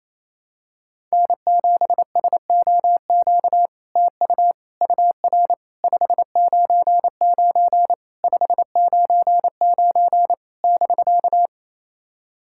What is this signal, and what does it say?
Morse code, heard as a keyed tone, about 28 wpm, 700 Hz. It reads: N7SOQ TU UR 599 599 <BK>
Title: SOTA QSO Practice 28wpm